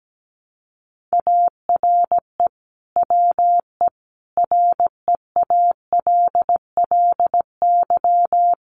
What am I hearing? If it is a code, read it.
Morse code, heard as a keyed tone, about 17 wpm, 700 Hz. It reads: ARE WE REALLY
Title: Sentences from Top 1400 Words 17wpm